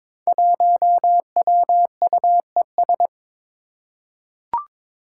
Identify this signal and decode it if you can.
Morse code, heard as a keyed tone, about 22 wpm, 700 Hz. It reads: 1WUES
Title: Five Challenging Letter-Number Combinations 22wpm